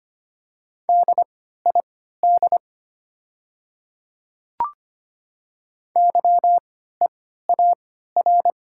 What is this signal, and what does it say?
Morse code, heard as a keyed tone, about 25 wpm, spaced as for 14 wpm, 700 Hz. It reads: DID YEAR
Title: 3x Character Spacing - Top 500 Words - 25wpm